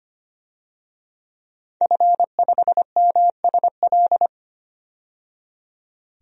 Morse code, heard as a keyed tone, about 25 wpm, 700 Hz. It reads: F5MSL
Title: DXCC-100 Call Signs - 5 Characters - 25wpm